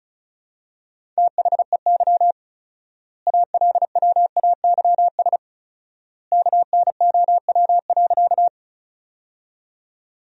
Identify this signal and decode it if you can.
Morse code, heard as a keyed tone, about 36 wpm, 700 Hz. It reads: THEY ALWAYS KNOW.
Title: 4x Word-Spacing - Sentences from Top 300 Words - 35wpm